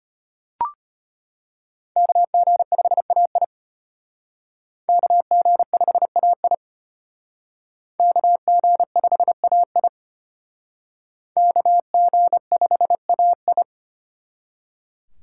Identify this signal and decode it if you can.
Morse code, heard as a keyed tone, about 33 wpm, 700 Hz. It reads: KG5AI KG5AI KG5AI KG5AI